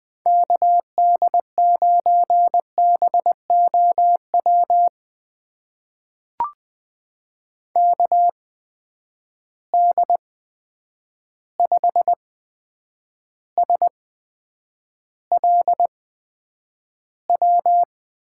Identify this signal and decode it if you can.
Morse code, heard as a keyed tone, about 20 wpm, spaced as for 9 wpm, 700 Hz. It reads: KD9BOW KD5SLW
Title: USA Call Signs - 6 Characters - 8x Character Spacing - 20wpm